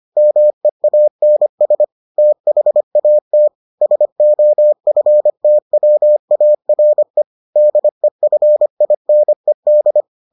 Morse code, heard as a keyed tone, about 25 wpm, 600 Hz. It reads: MEANS THAT SOFTWARE DEFINED